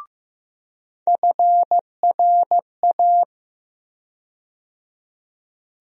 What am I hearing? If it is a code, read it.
Morse code, heard as a keyed tone, about 15 wpm, 700 Hz. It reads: FRA
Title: USA Call Sign Postfixes - 3 Letters 15wpm